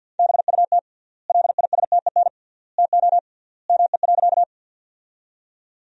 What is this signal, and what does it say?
Morse code, heard as a keyed tone, about 50 wpm, 700 Hz. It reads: BUT LISTEN TO ME.
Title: 3x Word-Spacing - Sentences from Top 400 Words - 50wpm